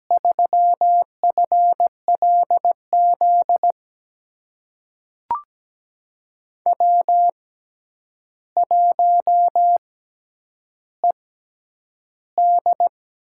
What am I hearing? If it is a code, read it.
Morse code, heard as a keyed tone, about 17 wpm, spaced as for 9 wpm, 700 Hz. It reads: 3FLZ W1ED
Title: USA Call Signs - 6x Character Spacing - 17wpm